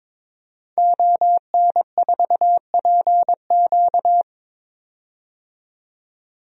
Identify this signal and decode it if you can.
Morse code, heard as a keyed tone, about 22 wpm, 700 Hz. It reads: ON4PQ
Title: DXCC-100 Call Signs - 5 Characters - 22wpm